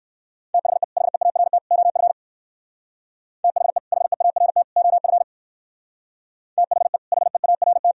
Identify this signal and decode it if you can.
Morse code, heard as a keyed tone, about 58 wpm, 700 Hz. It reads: THE HEART OF THE HEART OF THE HEART